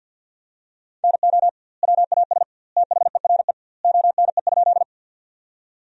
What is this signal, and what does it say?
Morse code, heard as a keyed tone, about 52 wpm, 700 Hz. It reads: NO WAS THERE ONE?
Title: 2x Word-Spacing - Sentences from Top 100 Words - 50wpm